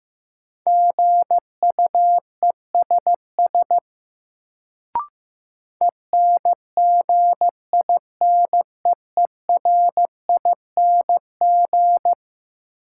Morse code, heard as a keyed tone, about 15 wpm, 700 Hz. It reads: GUESS ENGINEERING
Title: Sentences from Top 1500 Words - New Word Drill - 15wpm